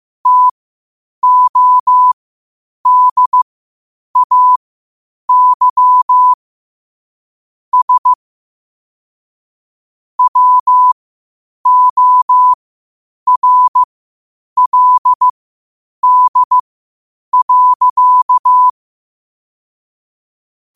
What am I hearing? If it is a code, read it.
Morse code, heard as a keyed tone, about 15 wpm, spaced as for 8 wpm, 1000 Hz. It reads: TODAY S WORLD.
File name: qotdfiles / 8wpm